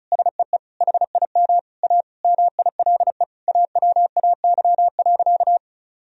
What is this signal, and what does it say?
Morse code, heard as a keyed tone, about 36 wpm, 700 Hz. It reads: SEE HIM A MILE AWAY.
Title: Sentences from Top 1000 Words - Review Part 1 - 35wpm